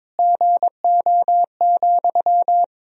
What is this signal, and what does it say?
Morse code, heard as a keyed tone, about 22 wpm, 700 Hz. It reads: GO,
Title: Sentences from Top 1000 Words - Review Part 4 - 22wpm